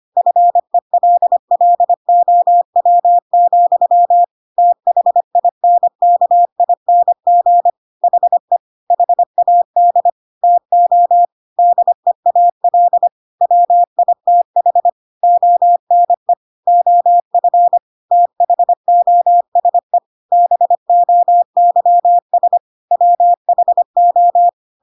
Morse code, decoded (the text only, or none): FELLOW, THINKING HE HAD TO DEAL WITH ONE OF THOSE BOYS WHO